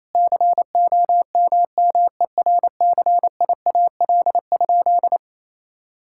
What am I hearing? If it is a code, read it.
Morse code, heard as a keyed tone, about 28 wpm, 700 Hz. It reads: COMMERCIAL?
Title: Sentences from Top 1000 Words - Review Part 5 - 28wpm